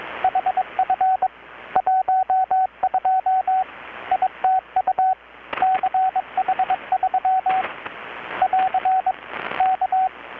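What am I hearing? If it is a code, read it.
Morse code, heard as a keyed tone, about 22 wpm, 750 Hz, with radio noise. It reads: HF 12 ITU CH3 <AR> K
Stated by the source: Morse from KPH's Night of Nights XX maritime radiotelegraphy event, received in Chiapas, Mexico about 3,800 km away on a Kenwood R-600 with a grounded 10 m long-wire antenna.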